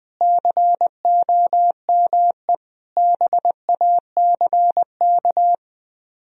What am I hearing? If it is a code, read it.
Morse code, heard as a keyed tone, about 20 wpm, 700 Hz. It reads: COME BACK